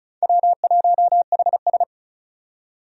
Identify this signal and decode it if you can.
Morse code, heard as a keyed tone, about 36 wpm, 700 Hz. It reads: W1HS
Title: USA Call Signs - 4 Characters - 35wpm